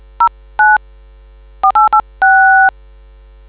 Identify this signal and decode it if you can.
DTMF, heard as telephone touch tones, tones of about 200 ms. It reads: 09 486